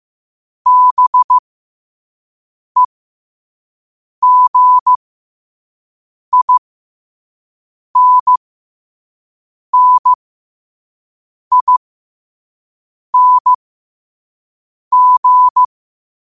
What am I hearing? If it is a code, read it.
Morse code, heard as a keyed tone, about 15 wpm, spaced as for 5 wpm, 1000 Hz. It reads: BEGINNING
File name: qotdfiles / 5wpm